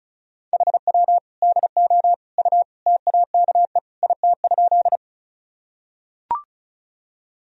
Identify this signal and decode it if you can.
Morse code, heard as a keyed tone, about 35 wpm, 700 Hz. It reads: HW DO U TAKE IT?
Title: Sentences from Top 200 Words and Abbreviations 35wpm